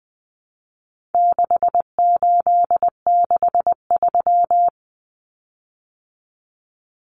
Morse code, heard as a keyed tone, about 20 wpm, 700 Hz. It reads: 6863
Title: Common Cut Numbers - 4-Digit 20wpm